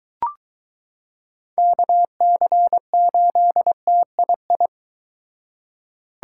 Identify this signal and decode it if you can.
Morse code, heard as a keyed tone, about 23 wpm, 700 Hz. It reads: KC8TII